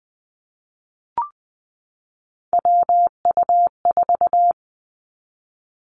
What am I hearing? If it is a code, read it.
Morse code, heard as a keyed tone, about 20 wpm, 700 Hz. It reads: WU4
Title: Three Challenging Letter-Number Combinations 20wpm